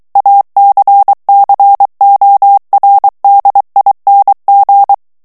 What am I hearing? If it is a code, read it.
Morse code, heard as a keyed tone, about 23 wpm, 800 Hz. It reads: ACCORDING
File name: morse mp3 / ANGLAIS texte clair 2 vitesse 22 mots minute